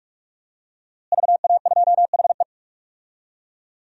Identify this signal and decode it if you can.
Morse code, heard as a keyed tone, about 41 wpm, 700 Hz. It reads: VA2HE
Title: DXCC-100 Call Signs - 5 Characters - Repeated 2x - 45wpm